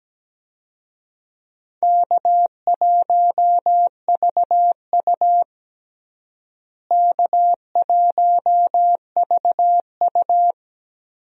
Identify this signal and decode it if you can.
Morse code, heard as a keyed tone, about 17 wpm, 700 Hz. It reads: K1VU K1VU
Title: USA Call Signs - 4 Characters - Repeated 3x - 17wpm